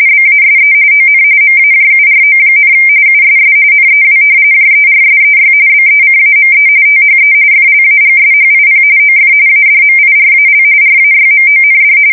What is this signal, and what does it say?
RTTY, about 45.45 baud. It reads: BASE ON MAJURO, USS NEW JERSEY BECAME THE FLAGSHIP OF ADMIRAL RAYMOND
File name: fsk170 / battleshipnj